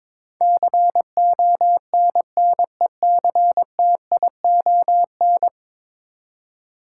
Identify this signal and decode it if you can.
Morse code, heard as a keyed tone, about 22 wpm, 700 Hz. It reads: CONNECTION
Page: morse-code-ninja-practice-22wpm